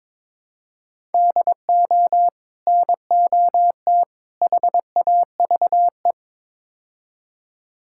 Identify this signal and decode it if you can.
Morse code, heard as a keyed tone, about 22 wpm, 700 Hz. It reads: DO NOT HAVE
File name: practice / 22wpm / Build A Phrase - 5-Word Phrases - 22wpm